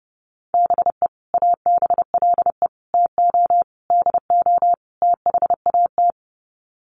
Morse code, heard as a keyed tone, about 30 wpm, 700 Hz. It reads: BE ABLE TO DO THAT
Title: Sets of 5 Words Encore2 30wpm